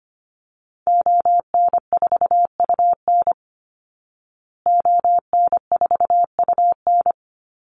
Morse code, heard as a keyed tone, about 25 wpm, 700 Hz. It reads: ON4UN ON4UN
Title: Call Signs Repeated 2x - 25wpm